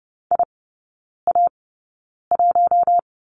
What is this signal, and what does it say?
Morse code, heard as a keyed tone, about 30 wpm, spaced as for 9 wpm, 700 Hz. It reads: IA1
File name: 081 Lesson 14e - Number 2 - Three Characters 30wpm